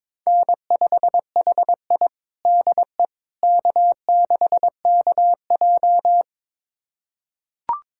Morse code, heard as a keyed tone, about 22 wpm, 700 Hz. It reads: N5HI DE K6KJ
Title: USA Callsign DE Callsign - 4 Characters - 22wpm